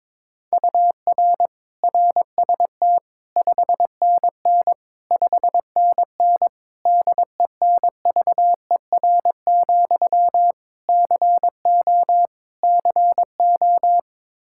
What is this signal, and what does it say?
Morse code, heard as a keyed tone, about 21 wpm, 700 Hz. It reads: UR RST 5NN 5NN DENVER, CO CO
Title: Practice QSO - Exchange 1 - Beginning - 22wpm